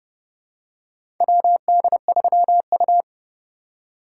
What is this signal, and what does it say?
Morse code, heard as a keyed tone, about 30 wpm, 700 Hz. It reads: WD3U
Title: USA Call Signs - 4 Characters - 4x Character Spacing - 30wpm